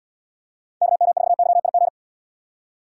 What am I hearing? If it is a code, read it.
Morse code, heard as a keyed tone, about 78 wpm, 700 Hz. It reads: KM4YIG